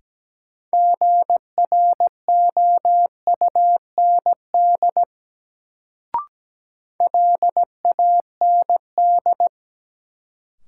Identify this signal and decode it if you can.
Morse code, heard as a keyed tone, about 17 wpm, 700 Hz. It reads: GROUND LAND
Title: Sentences from Top 500 Words - New Word Drill - 17wpm